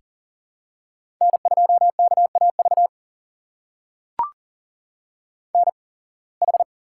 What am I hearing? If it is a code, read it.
Morse code, heard as a keyed tone, about 39 wpm, spaced as for 26 wpm, 700 Hz. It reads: N2KAV NH
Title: USA Call Signs - 5 Characters - 8x Character Spacing - 40wpm